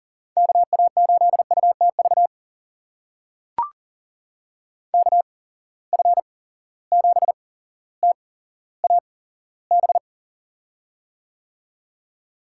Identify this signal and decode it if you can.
Morse code, heard as a keyed tone, about 39 wpm, spaced as for 17 wpm, 700 Hz. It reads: KA8UTV KF7TAB
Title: USA Call Signs - 6 Characters - 8x Character Spacing - 40wpm